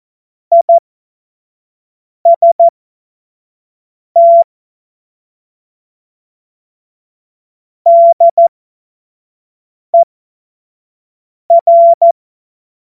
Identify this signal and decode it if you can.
Morse code, heard as a keyed tone, about 14 wpm, spaced as for 5 wpm, 680 Hz. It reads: IST DER
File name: Morsekurs / Witze / Witz 16. Februar 2021, 25 F70